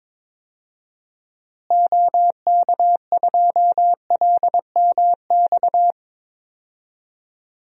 Morse code, heard as a keyed tone, about 22 wpm, 700 Hz. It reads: OK2LMX
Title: DXCC-100 Call Signs - 6 Characters - Repeated 3x - 22wpm